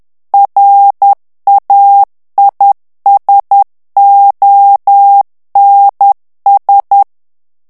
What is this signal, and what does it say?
Morse code, heard as a keyed tone, about 10 wpm, 800 Hz. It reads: RAISONS
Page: Français - texte en clair 1 - vitesse : 10 mots minute